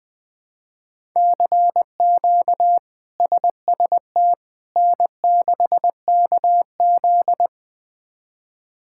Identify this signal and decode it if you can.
Morse code, heard as a keyed tone, about 20 wpm, 700 Hz. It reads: CQ SST N6KZ
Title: K1USN SST QSO Practice - 20wpm